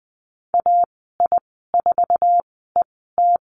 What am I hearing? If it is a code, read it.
Morse code, heard as a keyed tone, about 20 wpm, spaced as for 14 wpm, 700 Hz. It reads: AI4ET
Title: 048 Lesson 9g - Number 4 - Call Signs 20wpm